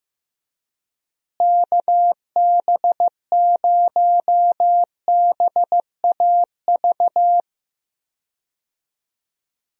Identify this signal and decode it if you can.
Morse code, heard as a keyed tone, about 15 wpm, 700 Hz. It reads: KB0BAV
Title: USA Call Signs - 6 Characters - Repeated 2x - 15wpm